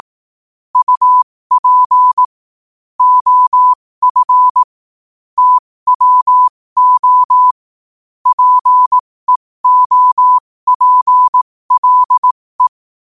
Morse code, heard as a keyed tone, about 18 wpm, spaced as for 15 wpm, 1000 Hz. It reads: UP OF TWO PEOPLE
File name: qotdfiles / 15wpm